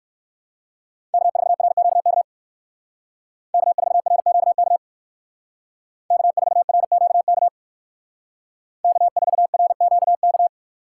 Morse code, heard as a keyed tone, about 59 wpm, 700 Hz. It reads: K4RQK K4RQK K4RQK K4RQK